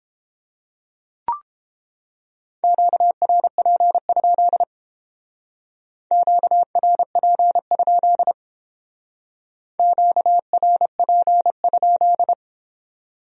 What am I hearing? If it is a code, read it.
Morse code, heard as a keyed tone, about 29 wpm, 700 Hz. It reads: QRP? QRP? QRP?